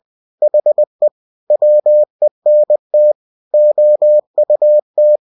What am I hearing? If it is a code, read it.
Morse code, heard as a keyed tone, about 20 wpm, 600 Hz. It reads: HE WENT OUT
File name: cw-books / winnie-the-pooh / winnie-20wpm-page-0096